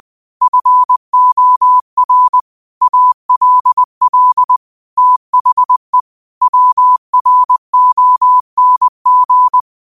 Morse code, heard as a keyed tone, about 20 wpm, 1000 Hz. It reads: FOR ALL THE WRONG
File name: qotdfiles / 20wpm